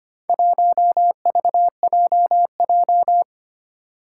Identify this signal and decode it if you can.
Morse code, heard as a keyed tone, about 25 wpm, 700 Hz. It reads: 1VJJ
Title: Four Challenging Letter-Number Combinations 25wpm